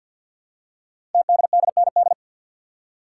Morse code, heard as a keyed tone, about 47 wpm, 700 Hz. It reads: TDDND
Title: Code Groups - 5 Letters - 50wpm